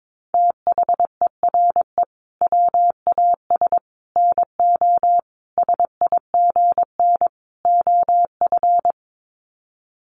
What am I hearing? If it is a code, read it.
Morse code, heard as a keyed tone, about 22 wpm, 700 Hz. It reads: THERE WAS NO SIGN OF
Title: Sets of 5 Words Encore2 22wpm